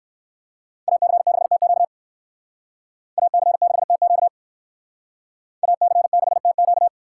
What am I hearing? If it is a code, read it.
Morse code, heard as a keyed tone, about 59 wpm, 700 Hz. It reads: AC6TQ AC6TQ AC6TQ